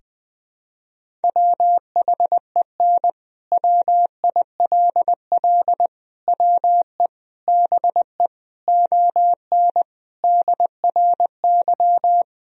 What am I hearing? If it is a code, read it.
Morse code, heard as a keyed tone, about 20 wpm, 700 Hz. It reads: WHEN WILL WE BE ON DRY
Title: Sentences from Top 1000 Words - Review Part 3 - 20wpm